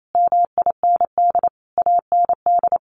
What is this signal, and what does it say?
Morse code, heard as a keyed tone, about 28 wpm, 700 Hz. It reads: MIND AND
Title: Sentences from Top 1400 Words 28wpm